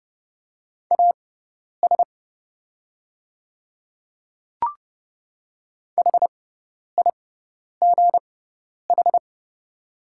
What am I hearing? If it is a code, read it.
Morse code, heard as a keyed tone, about 30 wpm, spaced as for 10 wpm, 690 Hz. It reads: AS HIGH
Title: 6x Character Spacing - Top 200 Words - 30wpm